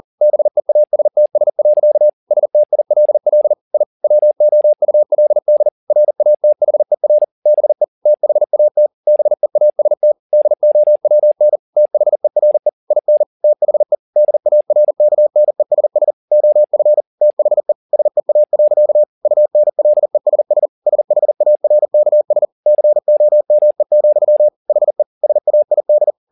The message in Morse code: BEASTS. STILL I WOULD RATHER BE THAT BEAST DOWN THERE IN THE DARKNESS OF THE SEA. UNLESS SHARKS COME, HE SAID

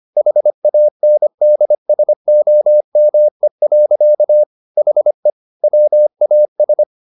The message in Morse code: HANDSOME. HE WAS

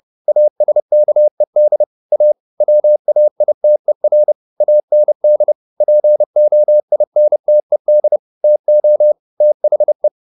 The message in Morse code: ASKED A WAITER AND POINTED TO THE

